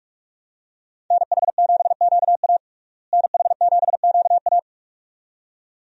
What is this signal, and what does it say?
Morse code, heard as a keyed tone, about 46 wpm, 700 Hz. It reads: NH7QA NH7QA
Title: USA Call Signs - 5 Characters - Repeated 2x - 45wpm